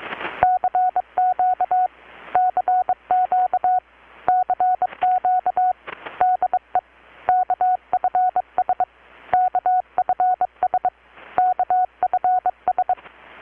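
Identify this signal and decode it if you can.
Morse code, heard as a keyed tone, about 22 wpm, 720 Hz. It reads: CQ CQ CQ DE KFS KFS KFS
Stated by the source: Morse from KPH's Night of Nights XX maritime radiotelegraphy event, received in Chiapas, Mexico about 3,800 km away on a Kenwood R-600 with a grounded 10 m long-wire antenna.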